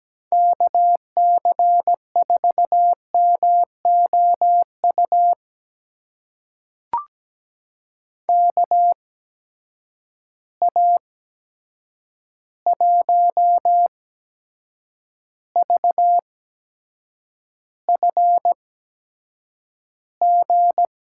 Morse code, heard as a keyed tone, about 17 wpm, spaced as for 7 wpm, 700 Hz. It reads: KC4MOU KA1VFG
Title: USA Call Signs - 6 Characters - 8x Character Spacing - 17wpm